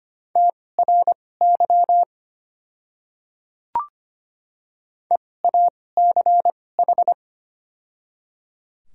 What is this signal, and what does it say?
Morse code, heard as a keyed tone, about 25 wpm, spaced as for 18 wpm, 700 Hz. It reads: TRY EACH
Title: 2x Character Spacing - Top 200 Words - 25wpm